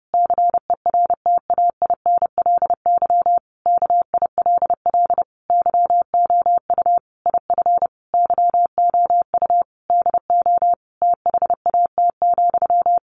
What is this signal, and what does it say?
Morse code, heard as a keyed tone, about 30 wpm, 700 Hz. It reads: CERTAINLY KILL YOU IF YOU DO THAT,
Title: Sentences from Top 1500 Words 30wpm